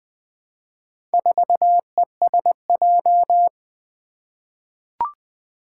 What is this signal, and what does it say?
Morse code, heard as a keyed tone, about 20 wpm, 700 Hz. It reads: 4ESJ
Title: Four Challenging Letter-Number Combinations 20wpm